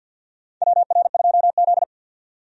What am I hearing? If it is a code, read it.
Morse code, heard as a keyed tone, about 52 wpm, 700 Hz. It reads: WR2Z
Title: USA Call Signs - 4 Characters - 50wpm